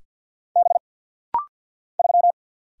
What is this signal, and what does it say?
Morse code, heard as a keyed tone, about 49 wpm, 700 Hz. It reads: B 3
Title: Single Letter-Number Alt - Rapid-Fire 50wpm